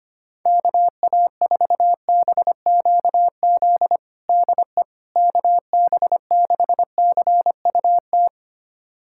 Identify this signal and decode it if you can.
Morse code, heard as a keyed tone, about 25 wpm, 700 Hz. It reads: KA4BQZ DE KB6CUT
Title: USA Callsign DE Callsign - 6 Characters - 25wpm